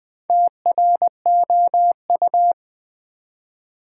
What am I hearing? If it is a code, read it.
Morse code, heard as a keyed tone, about 20 wpm, 700 Hz. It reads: TROU
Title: Build A Word - 2 Letters At A Time - 8-Letter Words - 20wpm